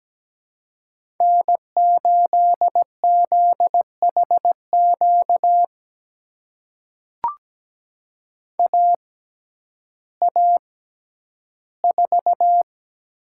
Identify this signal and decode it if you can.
Morse code, heard as a keyed tone, about 17 wpm, spaced as for 10 wpm, 700 Hz. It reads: N8ZHQ AA4
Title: USA Call Signs - 5 Characters - 6x Character Spacing - 17wpm